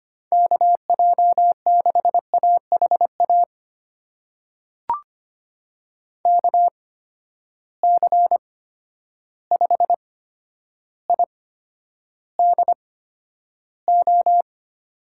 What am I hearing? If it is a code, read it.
Morse code, heard as a keyed tone, about 25 wpm, spaced as for 11 wpm, 700 Hz. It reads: KJ6AHA KC5IDO
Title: USA Call Signs - 6 Characters - 8x Character Spacing - 25wpm